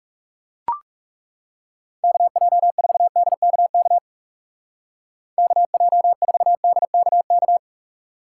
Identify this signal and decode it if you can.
Morse code, heard as a keyed tone, about 36 wpm, 700 Hz. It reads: KJ4DKK KJ4DKK
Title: Speed Racing - USA 2x3 Call Signs 30wpm